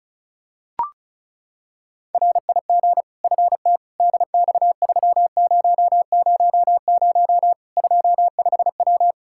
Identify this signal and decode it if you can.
Morse code, heard as a keyed tone, about 34 wpm, 700 Hz. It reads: RIG FT DX3000 25W
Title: Practice QSO - Exchange 2 - Info - 35wpm